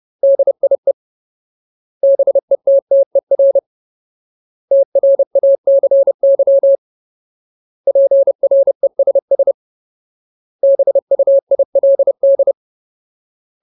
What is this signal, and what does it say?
Morse code, heard as a keyed tone, about 30 wpm, 550 Hz. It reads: DIE BETTER TRACY PRESS BUILD
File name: Ultimate Word List / 30WPM Ultimate